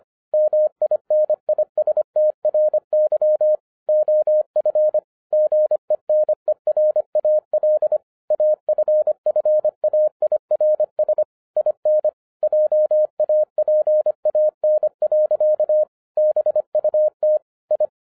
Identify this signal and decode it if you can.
Morse code, heard as a keyed tone, about 25 wpm, 610 Hz. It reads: MINISTRY OF GENERAL AFFAIRS IN JAPAN. BUT I